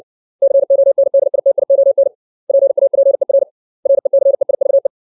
Morse code, heard as a keyed tone, about 60 wpm, 550 Hz. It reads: CONDITION PAPER RECEIVE
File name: 632 words / 632 Words 60WPM